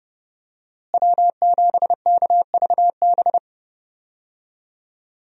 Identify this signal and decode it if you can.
Morse code, heard as a keyed tone, about 29 wpm, 700 Hz. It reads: W7KVB
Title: USA Call Signs - 5 Characters - Repeated 3x - 30wpm